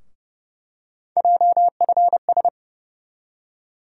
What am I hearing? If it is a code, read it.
Morse code, heard as a keyed tone, about 30 wpm, 700 Hz. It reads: JFS
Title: USA Call Sign Postfixes - 3 Letters 30wpm